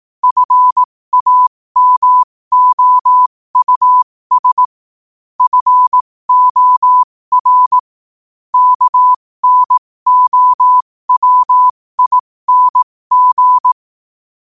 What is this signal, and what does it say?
Morse code, heard as a keyed tone, about 18 wpm, spaced as for 15 wpm, 1000 Hz. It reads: FAMOUS FOR KNOWING
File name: qotdfiles / 15wpm